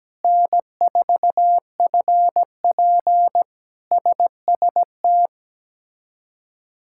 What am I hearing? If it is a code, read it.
Morse code, heard as a keyed tone, about 17 wpm, 700 Hz. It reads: N4FP SST
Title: K1USN SST QSO Practice - 17wpm